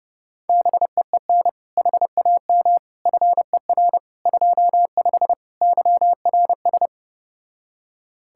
Morse code, heard as a keyed tone, about 32 wpm, 700 Hz. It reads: BEEN HAM FER 25 YRS